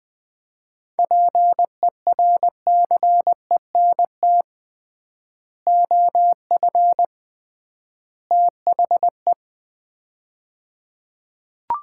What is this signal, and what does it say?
Morse code, heard as a keyed tone, about 20 wpm, 700 Hz. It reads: PERCENT OF THE
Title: 3x Word-Spacing - Sets of 3 Words - 20wpm